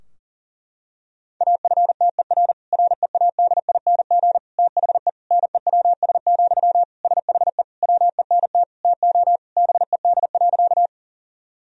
Morse code, heard as a keyed tone, about 40 wpm, 700 Hz. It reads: AFTER READING THE NEWS, SHE WENT TO BED.
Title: Sentences from Top 1000 Words - Review Part 3 - 40wpm